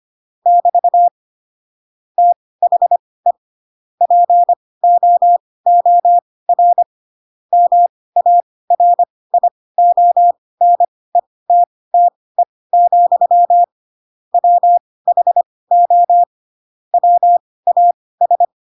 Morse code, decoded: <BT> THE POOR MARIONETTE, WHO WAS

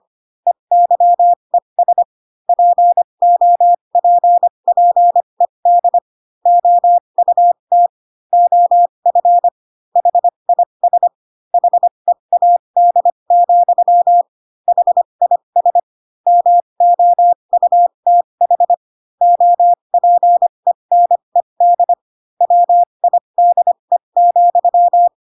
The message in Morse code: EYES POPPED OUT OF HIS HEAD, HIS MOUTH OPENED WIDE,

